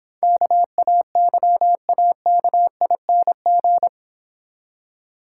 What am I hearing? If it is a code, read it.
Morse code, heard as a keyed tone, about 26 wpm, 700 Hz. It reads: KAYAKING